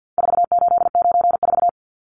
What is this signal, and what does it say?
Morse code, heard as a keyed tone, about 50 wpm, 700 Hz. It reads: 4894
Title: Numbers - 4-Digit 50wpm